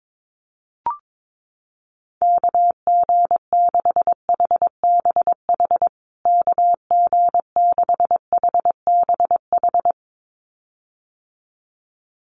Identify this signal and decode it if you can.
Morse code, heard as a keyed tone, about 22 wpm, 700 Hz. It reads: KG6HBH KG6HBH
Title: USA Call Signs Repeated 2x 22wpm